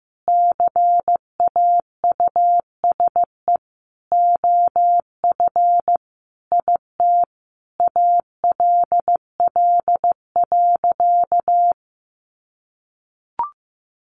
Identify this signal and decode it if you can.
Morse code, heard as a keyed tone, about 14 wpm, 700 Hz. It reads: CAUSE OF IT ALL.
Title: Sentences from Top 200 Words 15wpm